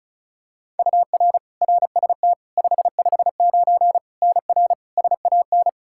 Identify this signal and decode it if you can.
Morse code, heard as a keyed tone, about 36 wpm, 700 Hz. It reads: UR RST 559 NR SAN
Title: K1USN SST QSO Practice 35wpm